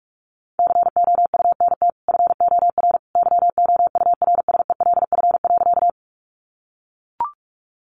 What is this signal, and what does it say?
Morse code, heard as a keyed tone, about 44 wpm, 700 Hz. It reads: COUNT FOR YOURSELF.
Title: Sentences from Top 1500 Words 45wpm